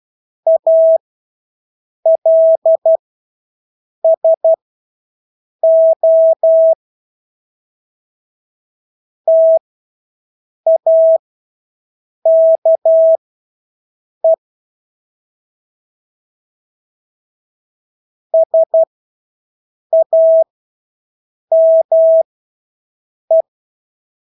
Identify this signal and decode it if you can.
Morse code, heard as a keyed tone, about 12 wpm, spaced as for 6 wpm, 650 Hz. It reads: ALSO TAKE SAME
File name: Engelske ord 12-6 wpm